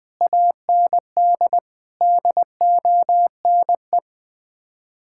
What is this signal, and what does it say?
Morse code, heard as a keyed tone, about 20 wpm, 700 Hz. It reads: AND DONE
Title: Sets of 6 Words - 20wpm